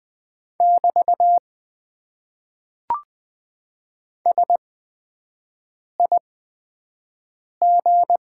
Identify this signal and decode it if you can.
Morse code, heard as a keyed tone, about 20 wpm, spaced as for 5 wpm, 700 Hz. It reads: <BT> SIG
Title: Top 90 QSO Elements - 8x Character Spacing - 20wpm